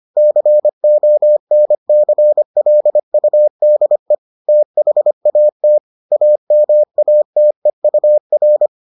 Morse code, heard as a keyed tone, about 25 wpm, 600 Hz. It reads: CONCLUDE THAT AMATEUR